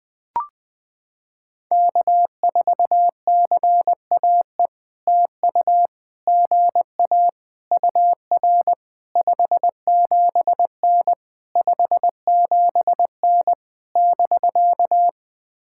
Morse code, heard as a keyed tone, about 21 wpm, 700 Hz. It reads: K4CAE TU GA UR 57N 57N <BK>
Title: POTA QSO Practice - 20wpm